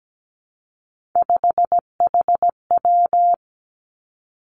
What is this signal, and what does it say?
Morse code, heard as a keyed tone, about 17 wpm, 700 Hz. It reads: 5HW
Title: Three Challenging Letter-Number Combinations 17wpm